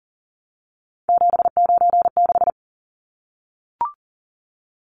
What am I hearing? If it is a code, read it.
Morse code, heard as a keyed tone, about 40 wpm, 700 Hz. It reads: 796
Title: Numbers - 3-Digit 40wpm